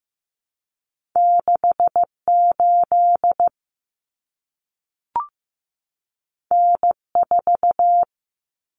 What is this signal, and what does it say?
Morse code, heard as a keyed tone, about 15 wpm, 700 Hz. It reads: 68 N4
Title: Common Cut Numbers - 2-Digit 15wpm